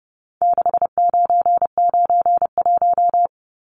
Morse code, heard as a keyed tone, about 30 wpm, 700 Hz. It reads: 6991
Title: Numbers - 4-Digit 30wpm